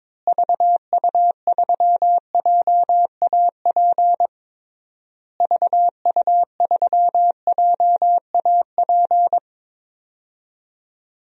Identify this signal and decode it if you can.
Morse code, heard as a keyed tone, about 22 wpm, 700 Hz. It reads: VU3JAP VU3JAP
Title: DXCC-100 Call Signs - 6 Characters - Repeated 3x - 22wpm